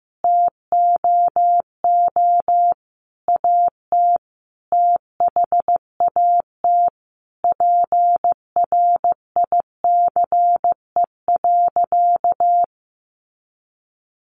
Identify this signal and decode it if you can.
Morse code, heard as a keyed tone, about 15 wpm, 700 Hz. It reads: TOO AT THAT PRICE.
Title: Sentences from Top 1200 Words 15wpm